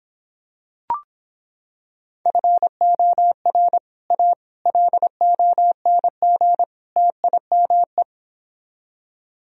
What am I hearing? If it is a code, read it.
Morse code, heard as a keyed tone, about 26 wpm, 700 Hz. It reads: FOR A LONG TIME